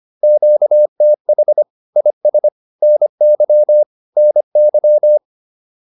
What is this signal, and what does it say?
Morse code, heard as a keyed tone, about 25 wpm, 600 Hz. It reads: QTH IS NY NY
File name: QSO106 25